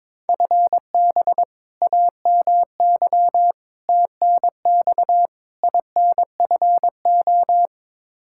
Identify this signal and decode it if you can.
Morse code, heard as a keyed tone, about 22 wpm, 700 Hz. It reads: FB AMY TNX INFO
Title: Practice QSO - Exchange 3 - More Info - 22wpm